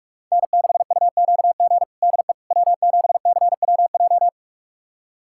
Morse code, heard as a keyed tone, about 46 wpm, 700 Hz. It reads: N6UQG DE W7CWJ
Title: USA Callsign DE Callsign - 5 Characters - 45wpm